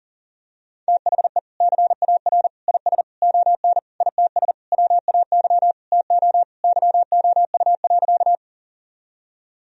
Morse code, heard as a keyed tone, about 40 wpm, 700 Hz. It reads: THE CAR IS ON ITS WAY TO YOU.
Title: Sentences from Top 1000 Words - Review Part 1 - 40wpm